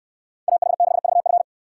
Morse code, heard as a keyed tone, about 64 wpm, 700 Hz. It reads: AH6VF